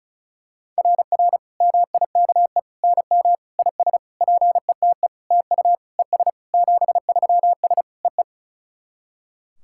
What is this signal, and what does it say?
Morse code, heard as a keyed tone, about 37 wpm, 700 Hz. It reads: RR MIKE NM IS PETE TU ES 73S EE
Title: SOTA QSO Practice 35wpm